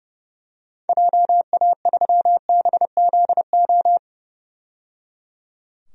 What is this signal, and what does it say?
Morse code, heard as a keyed tone, about 30 wpm, 700 Hz. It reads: JA3BZO
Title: DXCC-100 Call Signs - 6 Characters - 30wpm